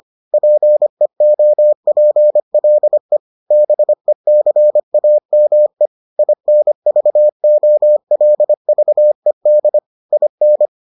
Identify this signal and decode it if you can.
Morse code, heard as a keyed tone, about 25 wpm, 600 Hz. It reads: PEOPLE BECAME INVOLVED IN